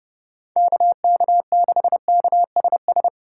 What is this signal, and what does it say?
Morse code, heard as a keyed tone, about 30 wpm, 700 Hz. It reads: KK6KSS
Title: USA Call Signs - 6 Characters - 2x Character Spacing - 30wpm